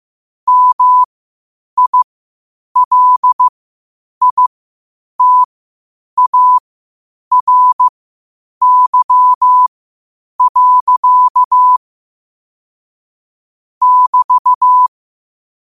Morse code, heard as a keyed tone, about 15 wpm, spaced as for 8 wpm, 1000 Hz. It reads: MILITARY. <BT>
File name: qotdfiles / 8wpm